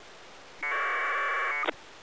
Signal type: packet radio (AX.25, as in APRS)